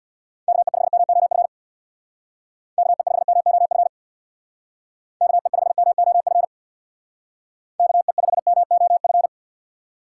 Morse code, heard as a keyed tone, about 58 wpm, 700 Hz. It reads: KE5MOF KE5MOF KE5MOF KE5MOF
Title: Speed Racing - USA Call Signs 50wpm